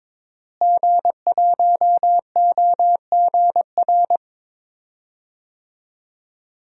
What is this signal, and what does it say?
Morse code, heard as a keyed tone, about 22 wpm, 700 Hz. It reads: G1OGR
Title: DXCC-100 Call Signs - 5 Characters - Repeated 2x - 22wpm